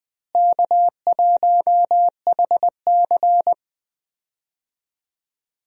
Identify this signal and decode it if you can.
Morse code, heard as a keyed tone, about 20 wpm, 700 Hz. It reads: K1HC
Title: USA Call Signs - 4 Characters - Repeated 2x - 20wpm